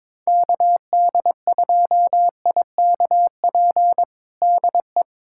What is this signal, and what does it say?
Morse code, heard as a keyed tone, about 22 wpm, 700 Hz. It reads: KD2IKP DE
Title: USA Callsign DE Callsign - 6 Characters - 22wpm